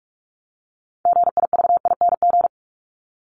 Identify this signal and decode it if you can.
Morse code, heard as a keyed tone, about 44 wpm, 700 Hz. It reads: GIVING
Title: ING Suffix - Top 100 Words with Base Words 45wpm